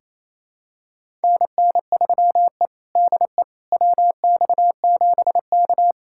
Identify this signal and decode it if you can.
Morse code, heard as a keyed tone, about 28 wpm, 700 Hz. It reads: NN3E DE WX7K
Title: USA Callsign DE Callsign - 4 Characters - 28wpm